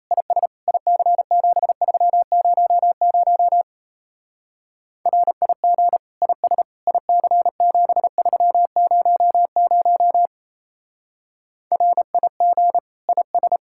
Morse code, heard as a keyed tone, about 32 wpm, 700 Hz. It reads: IS IC7300 RIG IS IC7300 RIG IS
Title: Speed-Racing Common QSO Phrases Encore - 28wpm